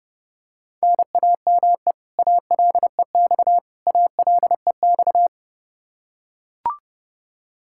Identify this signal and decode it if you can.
Morse code, heard as a keyed tone, about 28 wpm, 700 Hz. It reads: NAME ALEX ALEX
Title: Practice QSO - Exchange 1 - Beginning - 30wpm